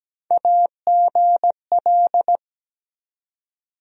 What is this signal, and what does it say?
Morse code, heard as a keyed tone, about 17 wpm, 700 Hz. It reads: AGL
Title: USA Call Sign Postfixes - 3 Letters 17wpm